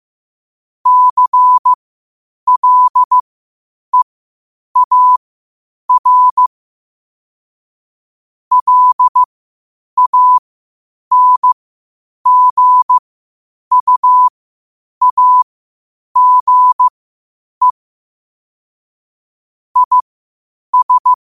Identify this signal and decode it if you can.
Morse code, heard as a keyed tone, about 15 wpm, spaced as for 8 wpm, 1000 Hz. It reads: CLEAR LANGUAGE IS
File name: qotdfiles / 8wpm